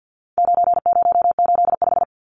Morse code, heard as a keyed tone, about 50 wpm, 700 Hz. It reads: 9085
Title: Numbers - 4-Digit 50wpm